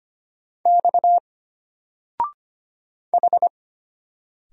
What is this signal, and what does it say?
Morse code, heard as a keyed tone, about 25 wpm, 700 Hz. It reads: X H
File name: Single-Letter-Number-25wpm